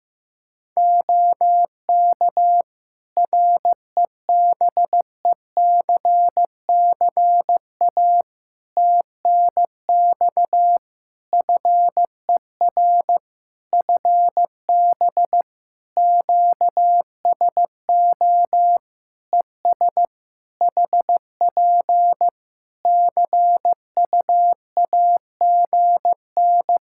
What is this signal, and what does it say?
Morse code, heard as a keyed tone, about 16 wpm, 700 Hz. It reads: OK REBECCA TNX FER FB QSO ES HP CUAGN
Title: Practice QSO - Exchange 4 - Ending - 15wpm